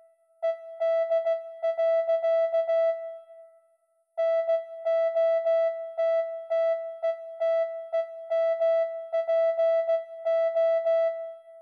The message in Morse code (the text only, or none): ED. NOTTETEMPO